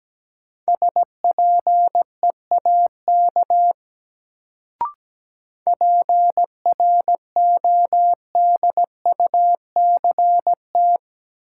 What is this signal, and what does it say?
Morse code, heard as a keyed tone, about 17 wpm, 700 Hz. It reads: SPEAK PRODUCT
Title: Sentences from Top 1000 Words - Review Part 2 - New Word Drill - 17wpm